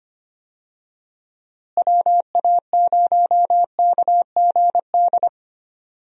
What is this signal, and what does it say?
Morse code, heard as a keyed tone, about 25 wpm, 700 Hz. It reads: WA0KGD
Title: USA Call Signs - 6 Characters - Repeated 2x - 25wpm